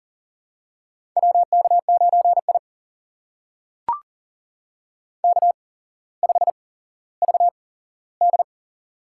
Morse code, heard as a keyed tone, about 42 wpm, spaced as for 17 wpm, 700 Hz. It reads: WK9I K5VD
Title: USA Call Signs - 4 Characters - 8x Character Spacing - 40wpm